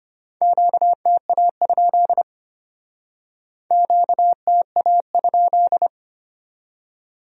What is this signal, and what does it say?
Morse code, heard as a keyed tone, about 34 wpm, 700 Hz. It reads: QTA? QTA?